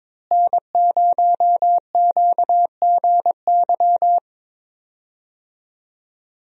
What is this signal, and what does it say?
Morse code, heard as a keyed tone, about 22 wpm, 700 Hz. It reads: N0QGY
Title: USA Call Signs - 5 Characters - Repeated 2x - 22wpm